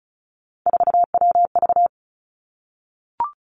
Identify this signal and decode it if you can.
Morse code, heard as a keyed tone, about 36 wpm, 700 Hz. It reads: 4WV
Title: Three Challenging Letter-Number Combinations 35wpm